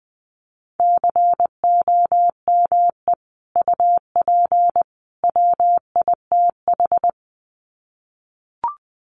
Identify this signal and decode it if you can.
Morse code, heard as a keyed tone, about 20 wpm, 700 Hz. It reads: COME UP WITH
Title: Sets of 3 Words 20wpm